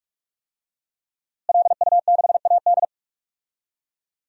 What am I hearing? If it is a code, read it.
Morse code, heard as a keyed tone, about 41 wpm, 700 Hz. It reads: LU6AD
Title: DXCC-100 Call Signs - 5 Characters - Repeated 3x - 45wpm